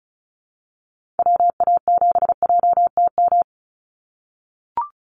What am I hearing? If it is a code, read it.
Morse code, heard as a keyed tone, about 34 wpm, 700 Hz. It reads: WA7JTM
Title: NA SOTA Chaser Callsigns 35wpm